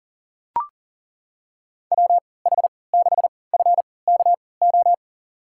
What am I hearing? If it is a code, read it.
Morse code, heard as a keyed tone, about 40 wpm, spaced as for 23 wpm, 700 Hz. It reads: WH6FKO
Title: USA Call Signs - 6 Characters - 3x Character Spacing - 40wpm